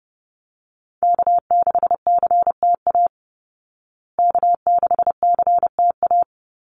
Morse code, heard as a keyed tone, about 30 wpm, 700 Hz. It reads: K6CTA K6CTA
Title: Call Signs Repeated 2x - 30wpm